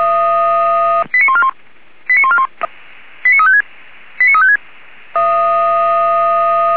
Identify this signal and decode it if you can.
Five-tone selective call (ZVEI), heard as quick runs of five tones, tones of about 80 ms. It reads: 89141 89141 89356 89356